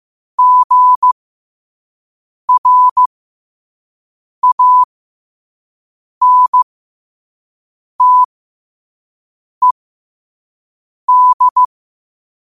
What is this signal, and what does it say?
Morse code, heard as a keyed tone, about 15 wpm, spaced as for 5 wpm, 1000 Hz. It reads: GRANTED
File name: qotdfiles / 5wpm